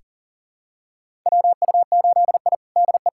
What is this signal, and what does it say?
Morse code, heard as a keyed tone, about 40 wpm, 700 Hz. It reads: WU8I DE
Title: USA Callsign DE Callsign - 4 Characters - 40wpm